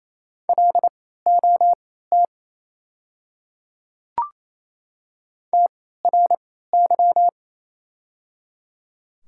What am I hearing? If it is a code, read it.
Morse code, heard as a keyed tone, about 27 wpm, spaced as for 16 wpm, 700 Hz. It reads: LOT TRY
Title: 3x Character Spacing - Top 500 Words - 28wpm